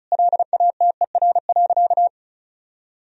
Morse code, heard as a keyed tone, about 34 wpm, 700 Hz. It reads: LATER.
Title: Sentences from Top 1000 Words - Review Part 4 - 35wpm